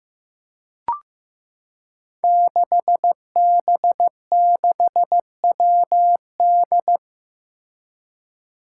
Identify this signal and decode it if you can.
Morse code, heard as a keyed tone, about 15 wpm, 700 Hz. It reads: 6B6WD
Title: Five Challenging Letter-Number Combinations 15wpm